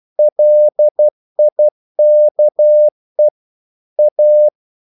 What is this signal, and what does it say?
Morse code, heard as a keyed tone, about 12 wpm, 600 Hz. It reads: LIKE A
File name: mp3 files / the gardener - 12wpm